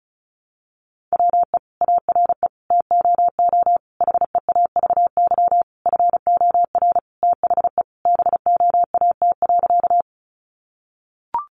Morse code, heard as a keyed tone, about 34 wpm, 700 Hz. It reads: WE ARE TOO HEAVY FOR THE BOAT.
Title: Sentences from Top 1100 Words 35wpm